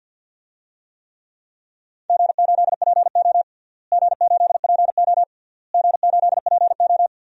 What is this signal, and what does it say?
Morse code, heard as a keyed tone, about 47 wpm, 700 Hz. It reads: G8PO G8PO G8PO
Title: DXCC-100 Call Signs - 4 Characters - Repeated 3x - 50wpm